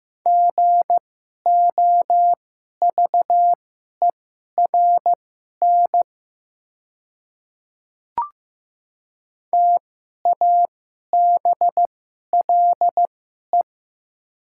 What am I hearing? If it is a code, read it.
Morse code, heard as a keyed tone, about 15 wpm, spaced as for 11 wpm, 700 Hz. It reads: GOVERN TABLE
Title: 2x Character Spacing - Top 500 Words - 15wpm